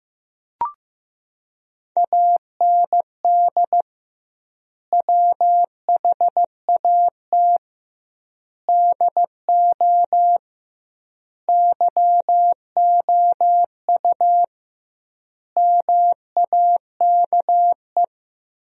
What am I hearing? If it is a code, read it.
Morse code, heard as a keyed tone, about 15 wpm, 700 Hz. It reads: AND WHAT DO YOU MAKE
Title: 2x Word-Spacing - Sentences from Top 100 Words - 15wpm